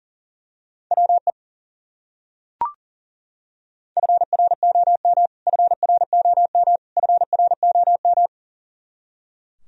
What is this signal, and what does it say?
Morse code, heard as a keyed tone, about 40 wpm, 700 Hz. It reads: WE FROM FROM FROM
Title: Top 100 Words Repeated 40wpm